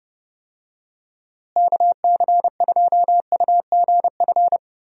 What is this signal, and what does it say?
Morse code, heard as a keyed tone, about 32 wpm, 700 Hz. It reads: KC2UGF
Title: USA Call Signs - 6 Characters - Repeated 2x - 30wpm